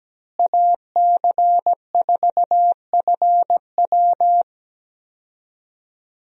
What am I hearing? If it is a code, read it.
Morse code, heard as a keyed tone, about 17 wpm, 700 Hz. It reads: AC4FW
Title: USA Call Signs - 5 Characters - Repeated 3x - 17wpm